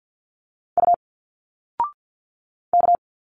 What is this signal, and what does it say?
Morse code, heard as a keyed tone, about 50 wpm, 700 Hz. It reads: U K
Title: Single Letters 50wpm